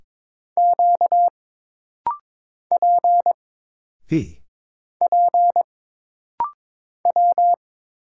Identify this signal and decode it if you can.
Morse code, heard as a keyed tone, about 22 wpm, 700 Hz. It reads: Q P P W
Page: This rapid-fire timing is geared to push your ICR (Instant Character Recognition) skills.